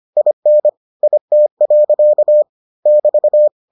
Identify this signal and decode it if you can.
Morse code, heard as a keyed tone, about 25 wpm, 600 Hz. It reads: IN IT. <BT>